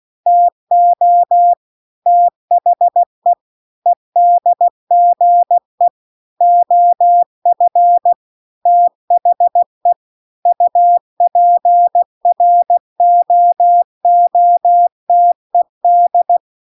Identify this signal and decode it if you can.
Morse code, heard as a keyed tone, about 16 wpm, 700 Hz. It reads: TO THE EDGE OF THE UPROOTED